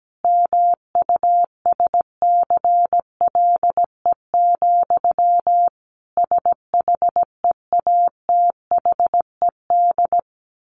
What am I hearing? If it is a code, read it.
Morse code, heard as a keyed tone, about 17 wpm, 700 Hz. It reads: MUSCLE, SHEATHED
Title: A Sound Of Thunder 17wpm